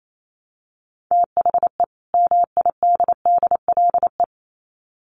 Morse code, heard as a keyed tone, about 28 wpm, 700 Hz. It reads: THE MIDDLE
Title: Sets of 2 Words 28wpm